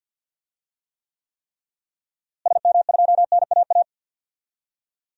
Morse code, heard as a keyed tone, about 52 wpm, 700 Hz. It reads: SM2NAA
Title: DXCC-100 Call Signs - 6 Characters - Repeated 3x - 50wpm